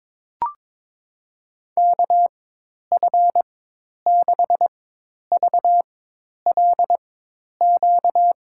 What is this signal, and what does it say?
Morse code, heard as a keyed tone, about 22 wpm, spaced as for 10 wpm, 700 Hz. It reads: KF6VLQ
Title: USA Call Signs - 6 Characters - 4x Character Spacing - 22wpm